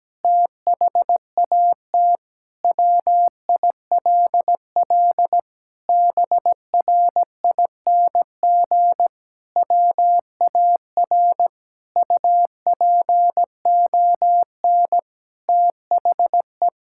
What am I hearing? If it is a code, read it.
Morse code, heard as a keyed tone, about 17 wpm, 700 Hz. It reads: THAT WILL BRING WAR UPON THE
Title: Sentences from Top 1000 Words - Review Part 6 - 17wpm